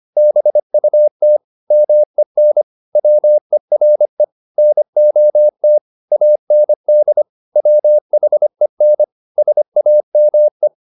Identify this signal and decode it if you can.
Morse code, heard as a keyed tone, about 25 wpm, 600 Hz. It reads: BUT MEN WERE NOT AND WHEN SAME
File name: the old man and the sea part 1 25wpm